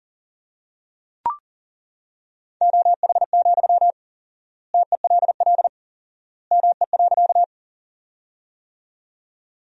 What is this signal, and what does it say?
Morse code, heard as a keyed tone, about 40 wpm, 700 Hz. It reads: OH, TELL ME.
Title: 4x Word-Spacing - Sentences from Top 500 Words - 40wpm